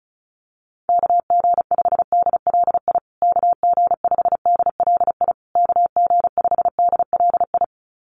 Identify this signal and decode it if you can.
Morse code, heard as a keyed tone, about 34 wpm, 700 Hz. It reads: KG5DLI KG5DLI KG5DLI
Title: USA Call Signs Repeated 3x 35wpm